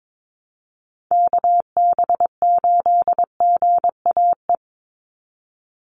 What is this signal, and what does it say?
Morse code, heard as a keyed tone, about 22 wpm, 700 Hz. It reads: KB8GAE
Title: Call Signs - 22wpm